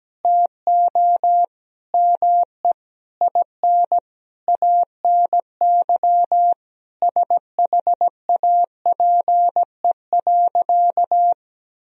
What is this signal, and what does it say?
Morse code, heard as a keyed tone, about 17 wpm, 700 Hz. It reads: TO ME IN ANY SHAPE.
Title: Sentences from Top 1000 Words - Review Part 3 - 17wpm